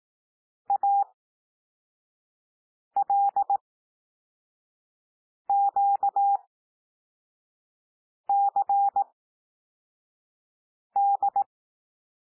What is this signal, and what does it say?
Morse code, heard as a keyed tone, about 18 wpm, spaced as for 4 wpm, 800 Hz. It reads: ALQCD
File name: cwmp3 / L2 SOUND LETTERS 18 wpm